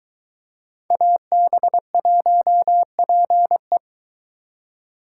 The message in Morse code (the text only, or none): AB1PE